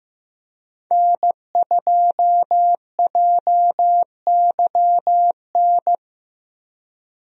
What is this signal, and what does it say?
Morse code, heard as a keyed tone, about 15 wpm, 700 Hz. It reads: N2JYN
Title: USA Call Signs - 5 Characters - Repeated 3x - 15wpm